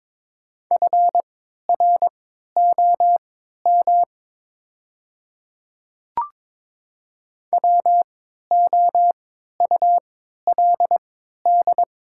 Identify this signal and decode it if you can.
Morse code, heard as a keyed tone, about 22 wpm, spaced as for 12 wpm, 690 Hz. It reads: FROM WOULD
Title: 3x Character Spacing - Top 200 Words - 22wpm